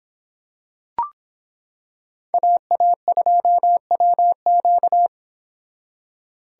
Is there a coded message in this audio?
Yes: Morse code from 2.3 to 5.1 s, heard as a keyed tone, about 25 wpm, 700 Hz. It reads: AA2WQ